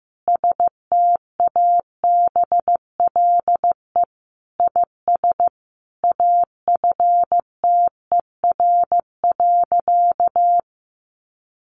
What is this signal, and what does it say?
Morse code, heard as a keyed tone, about 15 wpm, 700 Hz. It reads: STABLE IS AFTER.
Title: Sentences from Top 1500 Words 15wpm